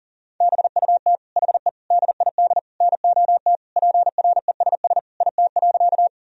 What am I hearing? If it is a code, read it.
Morse code, heard as a keyed tone, about 40 wpm, 700 Hz. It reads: BUT HE DID NOT PRESS IT.
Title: Sentences from Top 1000 Words - Review Part 1 - 40wpm